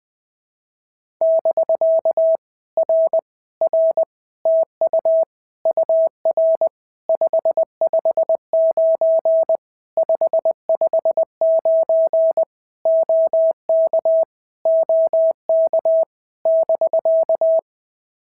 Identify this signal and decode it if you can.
Morse code, heard as a keyed tone, about 20 wpm, 650 Hz. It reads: <BK> R R TU UR 559 559 OK OK <BK>
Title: POTA QSO Practice - 20wpm